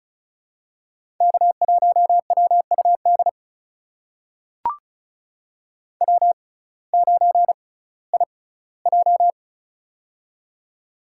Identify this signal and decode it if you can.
Morse code, heard as a keyed tone, about 36 wpm, spaced as for 19 wpm, 700 Hz. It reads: K1WUD W9IJ
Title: USA Call Signs - 6x Character Spacing - 35wpm